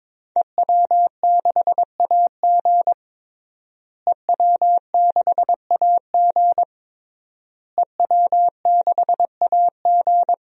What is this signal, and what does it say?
Morse code, heard as a keyed tone, about 22 wpm, 700 Hz. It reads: EW6AG EW6AG EW6AG
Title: DXCC-100 Call Signs - 5 Characters - Repeated 3x - 22wpm